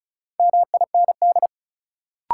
Morse code, heard as a keyed tone, about 36 wpm, 700 Hz. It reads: MIND
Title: Sets of 5 Words - New Word Drill - 35wpm